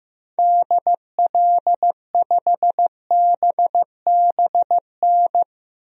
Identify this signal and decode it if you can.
Morse code, heard as a keyed tone, about 15 wpm, 700 Hz. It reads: DL5BBN
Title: DXCC-100 Call Signs - 6 Characters - 15wpm